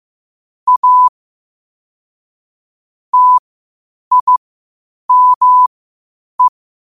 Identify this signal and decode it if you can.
Morse code, heard as a keyed tone, about 15 wpm, spaced as for 8 wpm, 1000 Hz. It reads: A TIME
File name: qotdfiles / 8wpm